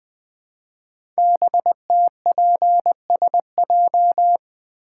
Morse code, heard as a keyed tone, about 20 wpm, 700 Hz. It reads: BTPSJ
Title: Code Groups - 5 Letters - 20wpm